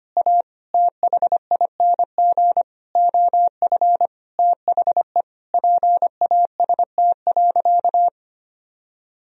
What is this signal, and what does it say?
Morse code, heard as a keyed tone, about 25 wpm, 700 Hz. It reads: A THING OF THE PAST.
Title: Sentences from Top 1000 Words - Review Part 3 - 25wpm